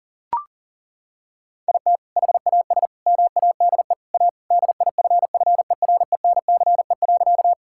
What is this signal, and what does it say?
Morse code, heard as a keyed tone, about 40 wpm, 700 Hz. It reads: IT HAS MADE A DIFFERENCE.
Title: Sentences from Top 1000 Words - Review Part 5 - 40wpm